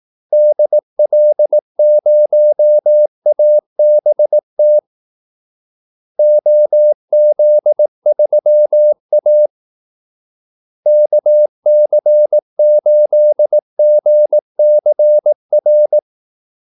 Morse code, heard as a keyed tone, about 18 wpm, 600 Hz. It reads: DL0ABT OZ3A KC8GCR
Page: Kaldesignaler 18 wpm